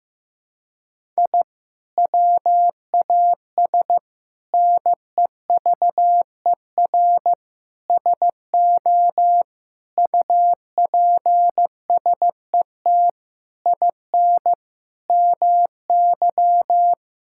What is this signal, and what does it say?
Morse code, heard as a keyed tone, about 16 wpm, 700 Hz. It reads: I WAS NEVER SO UPSET IN MY
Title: Sentences from Top 1000 Words - Review Part 5 - 15wpm